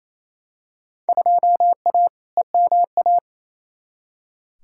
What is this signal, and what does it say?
Morse code, heard as a keyed tone, about 28 wpm, 700 Hz. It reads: 2A EMA
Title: Field Day Exchanges Encore - 28wpm